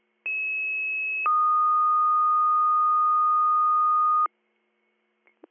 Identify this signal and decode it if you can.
Two-tone page, heard as a one-second tone, then a three-second tone, about 2600 then 1230 Hz. It is heard from 0.3 to 4.3 s.